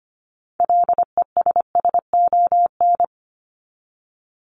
Morse code, heard as a keyed tone, about 25 wpm, 700 Hz. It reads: LESSON
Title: Top 1001-1250 Words 25wpm